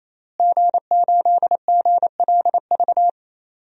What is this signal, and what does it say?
Morse code, heard as a keyed tone, about 28 wpm, 700 Hz. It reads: G8GLV
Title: DXCC-100 Call Signs - 5 Characters - Repeated 3x - 28wpm